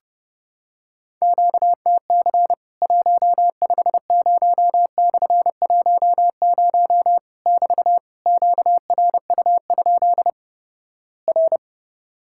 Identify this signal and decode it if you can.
Morse code, heard as a keyed tone, about 30 wpm, 700 Hz. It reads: QTC 150/10 <BT> QRU? R
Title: Worked All Europe DX Contest - QTC Exchange - 30wpm